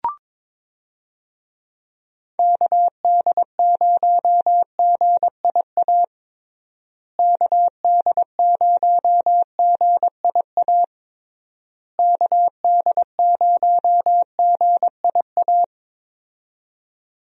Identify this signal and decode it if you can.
Morse code, heard as a keyed tone, about 22 wpm, 700 Hz. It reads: KD0GIA KD0GIA KD0GIA
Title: USA Call Signs - 6 Characters - Repeated 3x - 22wpm